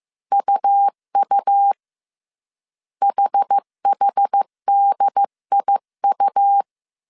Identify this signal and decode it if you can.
Morse code, heard as a keyed tone, about 15 wpm, 800 Hz. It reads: UU HHDIU
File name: CW / DET / 1 3 1500C S C 14wpm 2458